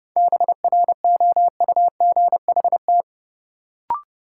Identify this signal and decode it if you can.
Morse code, heard as a keyed tone, about 30 wpm, 700 Hz. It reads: BROUGHT
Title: Sentences from Top 500 Words - New Word Drill - 30wpm